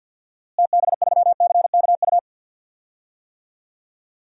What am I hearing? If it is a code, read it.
Morse code, heard as a keyed tone, about 51 wpm, 700 Hz. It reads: TB3CKU
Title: DXCC-100 Call Signs - 6 Characters - Repeated 2x - 50wpm